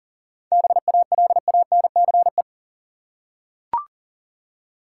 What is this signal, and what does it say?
Morse code, heard as a keyed tone, about 40 wpm, 700 Hz. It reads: BALANCE
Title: Seven Letter Words - 40wpm